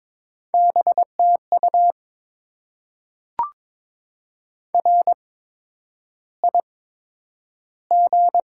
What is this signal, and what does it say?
Morse code, heard as a keyed tone, about 22 wpm, spaced as for 9 wpm, 700 Hz. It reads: BTU RIG
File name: Top 20 QSO Elements - 8x Character Spacing - 22wpm